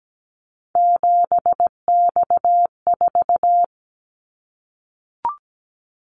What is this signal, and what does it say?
Morse code, heard as a keyed tone, about 17 wpm, 700 Hz. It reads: 7X4
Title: Callsign Prefixes for Top 100 DXCC - 17wpm